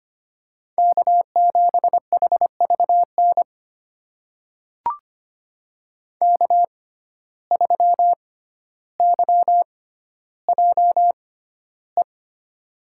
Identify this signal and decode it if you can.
Morse code, heard as a keyed tone, about 25 wpm, spaced as for 13 wpm, 700 Hz. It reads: K7HVN K3YJE
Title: USA Call Signs - 5 Characters - 6x Character Spacing - 25wpm